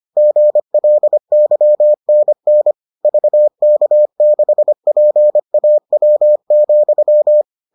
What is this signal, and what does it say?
Morse code, heard as a keyed tone, about 25 wpm, 600 Hz. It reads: GLYNN VK6PAW,